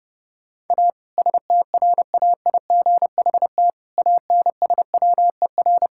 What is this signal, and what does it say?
Morse code, heard as a keyed tone, about 30 wpm, 700 Hz. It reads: A STRAIGHT ANSWER
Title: Sentences from Top 1000 Words - Review Part 6 - 30wpm